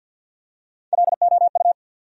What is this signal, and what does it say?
Morse code, heard as a keyed tone, about 53 wpm, 700 Hz. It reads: LOU
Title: USA Common Names - Max 4 Letters - 50wpm